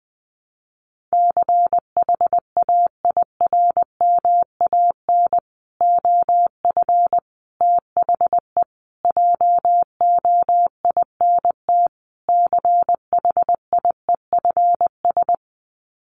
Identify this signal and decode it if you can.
Morse code, heard as a keyed tone, about 20 wpm, 700 Hz. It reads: CHAIRMAN OF THE JOINT CHIEFS
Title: Sets of 5 Words Encore 20wpm